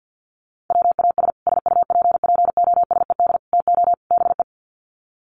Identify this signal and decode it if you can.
Morse code, heard as a keyed tone, about 50 wpm, 700 Hz. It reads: WAS SUPPOSED TO BE
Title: Sets of 4 Words 50wpm